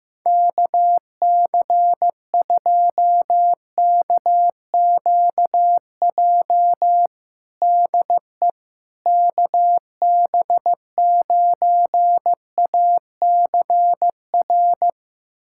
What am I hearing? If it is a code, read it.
Morse code, heard as a keyed tone, about 16 wpm, 700 Hz. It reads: KC2KQJ DE KB9ACR
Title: USA Callsign DE Callsign - 6 Characters - 15wpm